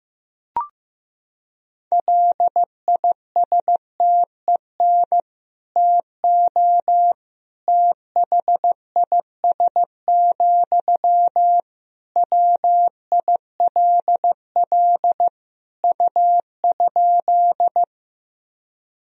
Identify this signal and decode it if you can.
Morse code, heard as a keyed tone, about 16 wpm, 700 Hz. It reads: LISTEN TO THIS, WILL U?
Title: Sentences from Top 400 Words and Abbreviations 15wpm